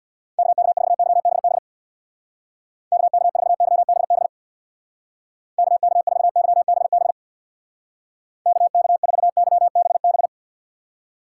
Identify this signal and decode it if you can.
Morse code, heard as a keyed tone, about 62 wpm, 700 Hz. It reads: KK4YBB KK4YBB KK4YBB KK4YBB
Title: Speed Racing - USA 2x3 Call Signs 50wpm